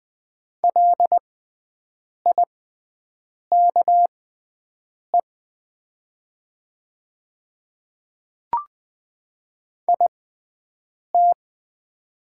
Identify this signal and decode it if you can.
Morse code, heard as a keyed tone, about 20 wpm, spaced as for 7 wpm, 690 Hz. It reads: LIKE IT
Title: 6x Character Spacing - Top 100 Words - 20wpm